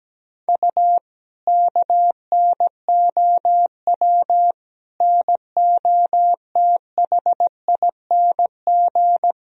Morse code, heard as a keyed tone, about 17 wpm, 700 Hz. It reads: U KNOW NOTHING
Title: Sentences from Top 500 Words and Abbreviations 17wpm